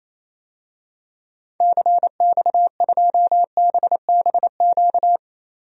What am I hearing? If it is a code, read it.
Morse code, heard as a keyed tone, about 28 wpm, 700 Hz. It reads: CX2BBQ
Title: DXCC-100 Call Signs - 6 Characters - 28wpm